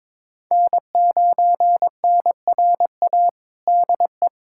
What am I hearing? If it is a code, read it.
Morse code, heard as a keyed tone, about 22 wpm, 700 Hz. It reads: N9NRA DE
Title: USA Callsign DE Callsign 22wpm